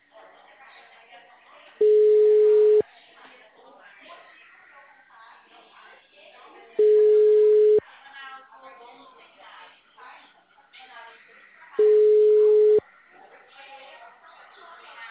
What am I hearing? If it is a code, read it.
Ringback tone (Europe, 425 Hz), with speech.
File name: out- -2015- - - .1609